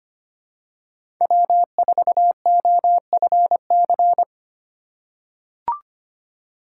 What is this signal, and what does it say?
Morse code, heard as a keyed tone, about 25 wpm, 700 Hz. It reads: W4OFC
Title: USA Call Signs - 2x Character Spacing - 25wpm